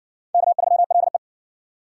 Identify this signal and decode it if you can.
Morse code, heard as a keyed tone, about 49 wpm, 700 Hz. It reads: K3LE
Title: Speed Racing - USA 1x2 Call Signs 40wpm